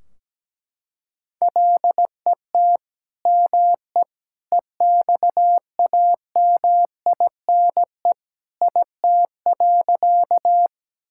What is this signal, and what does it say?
Morse code, heard as a keyed tone, about 17 wpm, 700 Hz. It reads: LET ME EXAMINE IT.
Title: Sentences from Top 1600 Words 17wpm